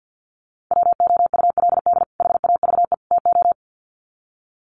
Morse code, heard as a keyed tone, about 50 wpm, 700 Hz. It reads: WOULD HAVE TO
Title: Sets of 3 Words 50wpm